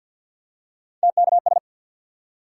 Morse code, heard as a keyed tone, about 53 wpm, 700 Hz. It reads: TKS
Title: Top 180 QSO Elements 50wpm